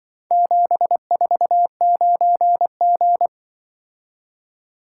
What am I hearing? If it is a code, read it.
Morse code, heard as a keyed tone, about 23 wpm, 700 Hz. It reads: 749G
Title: Speed-Racing - Four Letter-Number Combinations - 20wpm